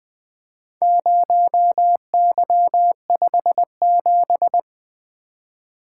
Morse code, heard as a keyed tone, about 20 wpm, 700 Hz. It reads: 0Y57
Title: Four Letter-Number Combinations - 20wpm